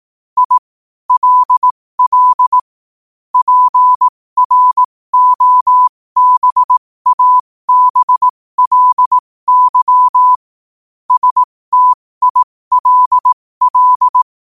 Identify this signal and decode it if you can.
Morse code, heard as a keyed tone, about 18 wpm, spaced as for 15 wpm, 1000 Hz. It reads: I LL PROBABLY STILL
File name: qotdfiles / 15wpm